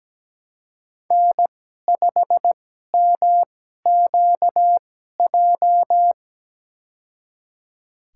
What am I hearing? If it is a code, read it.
Morse code, heard as a keyed tone, about 17 wpm, spaced as for 12 wpm, 700 Hz. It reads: N5MQJ
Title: USA Call Signs - 5 Characters - 2x Character Spacing - 17wpm